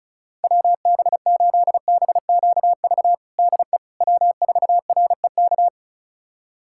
Morse code, heard as a keyed tone, about 34 wpm, 700 Hz. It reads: WB8BQV DE W4REK
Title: USA Callsign DE Callsign 35wpm